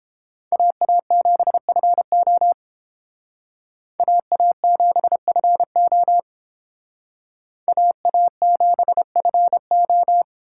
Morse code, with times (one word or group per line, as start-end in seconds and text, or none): 0.5-2.5 AA7FO
4.0-6.2 AA7FO
7.7-10.2 AA7FO